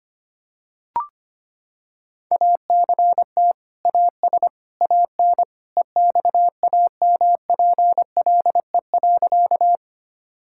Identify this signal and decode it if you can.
Morse code, heard as a keyed tone, about 25 wpm, 700 Hz. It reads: ACT AS AN EXAMPLE.
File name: practice / 25wpm / Sentences with 2-Word Phrases - 25wpm